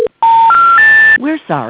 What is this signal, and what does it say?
Special information tone, with speech.